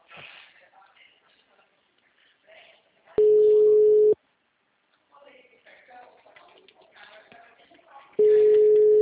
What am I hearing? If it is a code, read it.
Ringback tone (Europe, 425 Hz), with speech.
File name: out- -2001- - - .52213